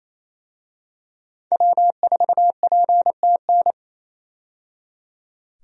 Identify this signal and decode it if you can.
Morse code, heard as a keyed tone, about 28 wpm, 700 Hz. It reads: W4PTN
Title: USA Call Signs - 5 Characters - 28wpm